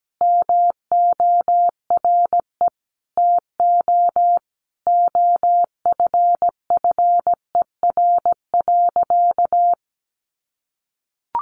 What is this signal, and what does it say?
Morse code, heard as a keyed tone, about 17 wpm, 700 Hz. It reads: MORE TO OFFER.
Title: Sentences from Top 1100 Words 17wpm